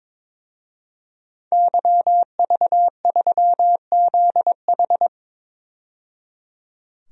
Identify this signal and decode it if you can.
Morse code, heard as a keyed tone, about 22 wpm, 700 Hz. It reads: YV3ZH
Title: DXCC-100 Call Signs - 5 Characters - 22wpm